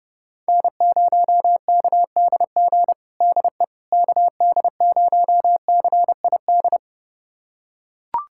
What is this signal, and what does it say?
Morse code, heard as a keyed tone, about 30 wpm, 700 Hz. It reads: N0KDG DE KD0CID
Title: USA Callsign DE Callsign 30wpm